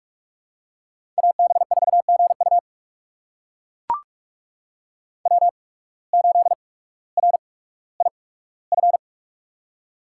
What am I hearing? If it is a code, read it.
Morse code, heard as a keyed tone, about 44 wpm, spaced as for 19 wpm, 700 Hz. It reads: AB4GU W8RIF
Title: USA Call Signs - 5 Characters - 8x Character Spacing - 45wpm